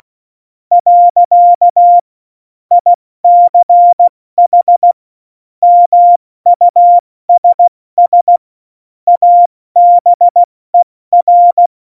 Morse code, heard as a keyed tone, about 16 wpm, spaced as for 14 wpm, 700 Hz. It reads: . ICH MUSS ABER
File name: Morsekurs / Witze / Witz 16. Juli 2019, 70 F80